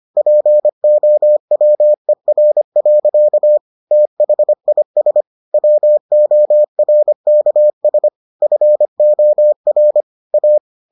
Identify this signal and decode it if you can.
Morse code, heard as a keyed tone, about 25 wpm, 600 Hz. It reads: POWER. THIS WORKS FOR A